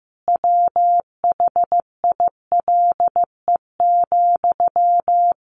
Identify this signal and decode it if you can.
Morse code, heard as a keyed tone, about 15 wpm, 700 Hz. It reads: WHILE,
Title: Sentences from Top 1100 Words 15wpm